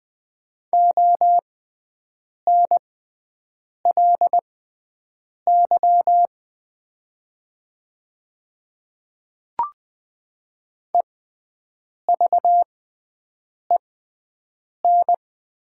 Morse code, heard as a keyed tone, about 20 wpm, spaced as for 7 wpm, 700 Hz. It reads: ONLY EVEN
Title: 6x Character Spacing - Top 300 Words - 20wpm